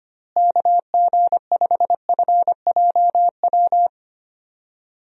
Morse code, heard as a keyed tone, about 25 wpm, 700 Hz. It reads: KG5FJW
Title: USA Call Signs - 6 Characters - Repeated 2x - 25wpm